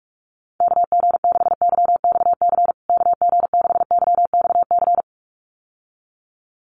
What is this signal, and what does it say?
Morse code, heard as a keyed tone, about 44 wpm, 700 Hz. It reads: KG6YXC KG6YXC
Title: USA Call Signs Repeated 2x 45wpm